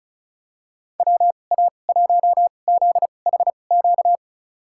Morse code, heard as a keyed tone, about 36 wpm, spaced as for 25 wpm, 700 Hz. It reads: WA1ZHQ
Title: USA Call Signs - 6 Characters - 2x Character Spacing - 35wpm